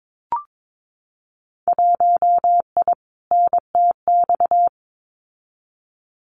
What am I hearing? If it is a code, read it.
Morse code, heard as a keyed tone, about 22 wpm, 700 Hz. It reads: 1I NTX
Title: Winter Field Day Exchanges Encore 22wpm